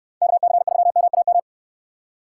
Morse code, heard as a keyed tone, about 69 wpm, 700 Hz. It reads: KC3MAM